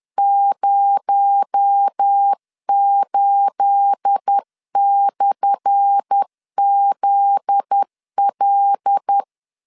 Morse code, heard as a keyed tone, about 10 wpm, 800 Hz. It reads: 08/ZL
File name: LEZIONI / 10 SIGNS / 10A SLASH 10wpm 2808